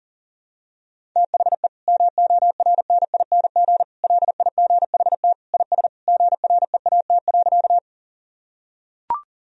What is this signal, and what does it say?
Morse code, heard as a keyed tone, about 40 wpm, 700 Hz. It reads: THE MORNING LIGHT IS GREAT.
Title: Sentences from Top 1000 Words - Review Part 1 - 40wpm